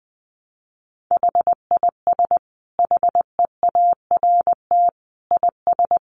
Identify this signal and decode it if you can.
Morse code, heard as a keyed tone, about 20 wpm, 700 Hz. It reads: HIS HEART IS
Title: Sentences from Top 1200 Words 20wpm